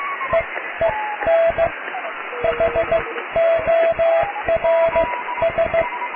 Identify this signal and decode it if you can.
Morse code, heard as a keyed tone, about 15 wpm, spaced as for 13 wpm, 660 Hz, with radio noise. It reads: EEN HORS
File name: cw / tekst(a)+QRM